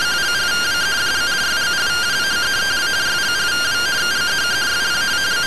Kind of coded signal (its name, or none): two-tone teleprinter signal